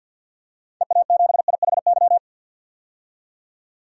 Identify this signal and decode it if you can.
Morse code, heard as a keyed tone, about 47 wpm, 700 Hz. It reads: EA7IHY
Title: DXCC-100 Call Signs - 6 Characters - 50wpm